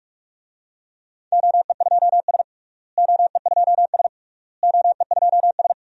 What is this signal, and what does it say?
Morse code, heard as a keyed tone, about 43 wpm, 690 Hz. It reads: OE2S OE2S OE2S
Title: DXCC-100 Call Signs - 4 Characters - Repeated 3x - 45wpm